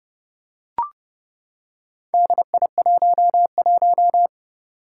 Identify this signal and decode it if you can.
Morse code, heard as a keyed tone, about 30 wpm, 700 Hz. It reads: DI11
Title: Four Challenging Letter-Number Combinations 30wpm